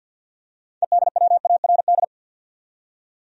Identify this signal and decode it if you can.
Morse code, heard as a keyed tone, about 52 wpm, 700 Hz. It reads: EDWARD
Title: USA Common Names - 50wpm